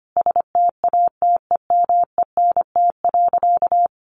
Morse code, heard as a keyed tone, about 25 wpm, 700 Hz. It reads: STATEMENT.
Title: Sentences from Top 1200 Words 25wpm